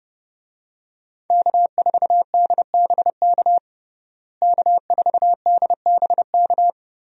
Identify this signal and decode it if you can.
Morse code, heard as a keyed tone, about 32 wpm, 700 Hz. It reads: K4DBK K4DBK
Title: USA Call Signs - 5 Characters - Repeated 2x - 30wpm